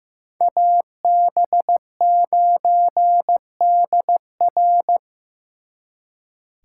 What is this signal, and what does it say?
Morse code, heard as a keyed tone, about 18 wpm, 700 Hz. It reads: AB9DR
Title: Speed Racing - USA 2x2 Call Signs 15wpm